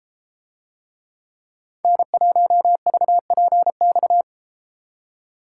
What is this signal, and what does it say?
Morse code, heard as a keyed tone, about 33 wpm, 700 Hz. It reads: N1VPX